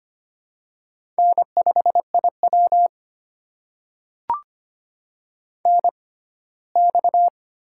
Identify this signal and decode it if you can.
Morse code, heard as a keyed tone, about 25 wpm, spaced as for 17 wpm, 700 Hz. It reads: N5IW NX
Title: USA Call Signs - 4 Characters - 6x Character Spacing - 25wpm